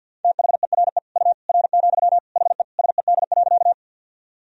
Morse code, heard as a keyed tone, about 51 wpm, 700 Hz. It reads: THERE U R, HE SED.
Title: Sentences from Top 100 Words and Abbreviations 50wpm